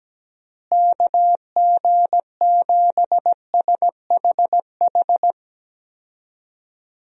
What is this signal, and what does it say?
Morse code, heard as a keyed tone, about 17 wpm, 700 Hz. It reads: KG7SHH
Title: USA Call Signs - 6 Characters - Repeated 3x - 17wpm